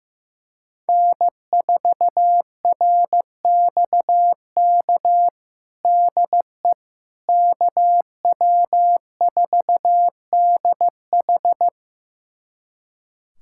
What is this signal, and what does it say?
Morse code, heard as a keyed tone, about 15 wpm, 700 Hz. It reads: N4RXK DE KW4DH
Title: USA Callsign DE Callsign - 5 Characters - 15wpm